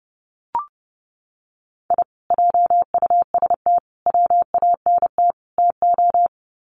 Morse code, heard as a keyed tone, about 30 wpm, 700 Hz. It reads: I JUST WANT TO
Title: Sets of 4 Words 30wpm